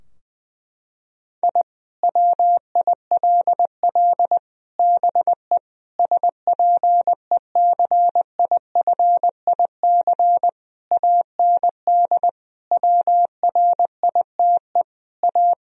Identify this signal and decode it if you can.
Morse code, heard as a keyed tone, about 19 wpm, 700 Hz. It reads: I WILL BE SPECIFIC AND WRITE A
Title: Sentences from Top 1000 Words - Review Part 3 - 20wpm